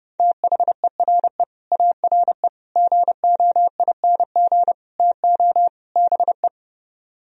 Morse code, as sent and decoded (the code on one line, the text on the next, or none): - .... .-. / .- .-. / --. --- .. -. --. / - --- / -... .
THERE ARE GOING TO BE